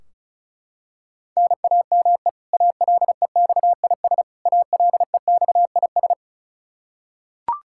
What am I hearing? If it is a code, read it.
Morse code, heard as a keyed tone, about 34 wpm, 700 Hz. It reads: NAME ALEXIS ALEXIS
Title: Practice QSO - Exchange 1 - Beginning - 35wpm